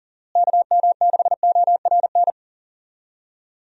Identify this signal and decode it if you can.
Morse code, heard as a keyed tone, about 40 wpm, 700 Hz. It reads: KM6ORN
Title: USA Call Signs - 6 Characters - Repeated 2x - 40wpm